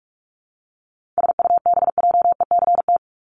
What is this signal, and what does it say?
Morse code, heard as a keyed tone, about 44 wpm, 700 Hz. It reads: SUBJECT
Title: Top 501-750 Words 45wpm